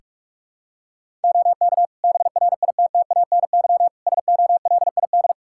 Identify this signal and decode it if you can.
Morse code, heard as a keyed tone, about 46 wpm, 700 Hz. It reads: OK BRITTANY SOLID
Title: K1USN SST QSO Practice 45wpm